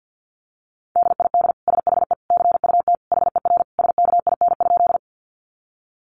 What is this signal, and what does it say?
Morse code, heard as a keyed tone, about 50 wpm, 700 Hz. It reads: DID SHE CUT HER SKIN?
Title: Sentences from Top 1300 Words 50wpm